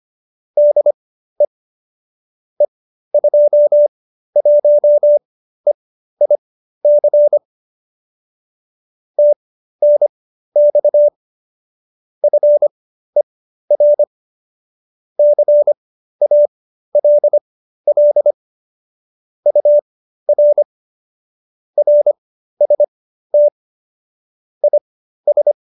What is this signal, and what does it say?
Morse code, heard as a keyed tone, about 25 wpm, spaced as for 13 wpm, 590 Hz. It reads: DE E21EIC TNX FER CALL UR RST IS
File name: qso208b 13